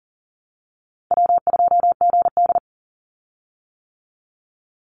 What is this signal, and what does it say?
Morse code, heard as a keyed tone, about 39 wpm, 700 Hz. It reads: W2GD
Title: Easy Call Signs - 40wpm